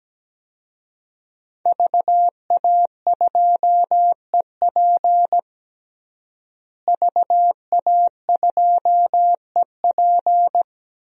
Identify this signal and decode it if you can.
Morse code, heard as a keyed tone, about 17 wpm, 700 Hz. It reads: VA2EP VA2EP
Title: DXCC-100 Call Signs - 5 Characters - Repeated 2x - 17wpm